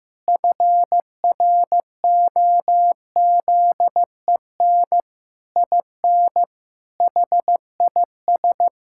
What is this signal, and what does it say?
Morse code, heard as a keyed tone, about 15 wpm, 700 Hz. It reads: FROZEN IN HIS
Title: Pangram Challenge v2 - 15wpm